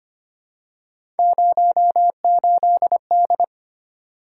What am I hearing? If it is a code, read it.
Morse code, heard as a keyed tone, about 25 wpm, 700 Hz. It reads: 08D
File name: Three Letter-Number Combinations - 25wpm